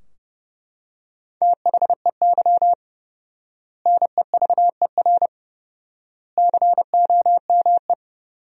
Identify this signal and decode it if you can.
Morse code, heard as a keyed tone, about 30 wpm, 700 Hz. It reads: THEY NEVER COME
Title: 4x Word-Spacing - Sentences from Top 300 Words - 30wpm